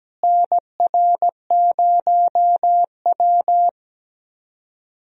Morse code, heard as a keyed tone, about 17 wpm, 700 Hz. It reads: NR0W
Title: USA Call Signs - 4 Characters - Repeated 2x - 17wpm